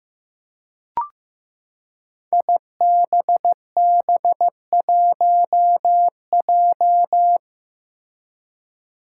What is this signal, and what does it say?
Morse code, heard as a keyed tone, about 15 wpm, 700 Hz. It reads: IBB1J
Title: Five Challenging Letter-Number Combinations 15wpm